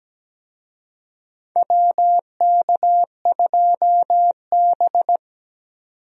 Morse code, heard as a keyed tone, about 17 wpm, 700 Hz. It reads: WK2B
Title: USA Call Signs - 4 Characters - 17wpm